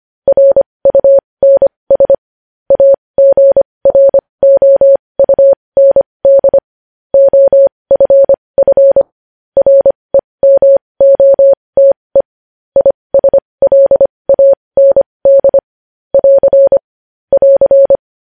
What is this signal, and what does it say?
Morse code, heard as a keyed tone, about 25 wpm, spaced as for 20 wpm, 550 Hz. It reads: RUNS AGROUND OFF REMOTE ISLAND <AR> <AR>
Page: news-headlines-in-morse-code-at-20-wpm